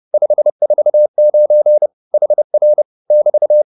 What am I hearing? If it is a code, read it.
Morse code, heard as a keyed tone, about 30 wpm, 600 Hz. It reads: 549 HR <BT>
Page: advanced-practice-files